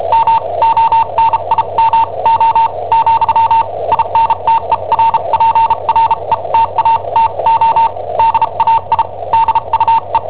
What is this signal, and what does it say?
Morse code, heard as a keyed tone, about 34 wpm, spaced as for 28 wpm, 910 Hz. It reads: MONIMO , INTERPRETATO DAI DUE
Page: Cassetta esercizi preparata da I7OHP per IW5ELP